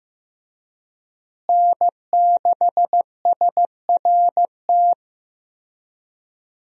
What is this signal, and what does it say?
Morse code, heard as a keyed tone, about 15 wpm, 700 Hz. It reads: N6SRT
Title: USA Call Signs - 5 Characters - Repeated 2x - 15wpm